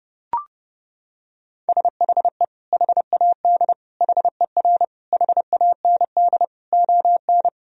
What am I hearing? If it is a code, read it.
Morse code, heard as a keyed tone, about 30 wpm, 700 Hz. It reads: SHE HAD HER HAND ON
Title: Sentences from Top 1000 Words - Review Part 2 - 30wpm